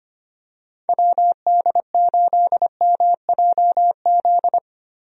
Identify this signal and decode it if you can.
Morse code, heard as a keyed tone, about 25 wpm, 700 Hz. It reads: WD8MJZ
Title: USA Call Signs - 6 Characters - Repeated 3x - 25wpm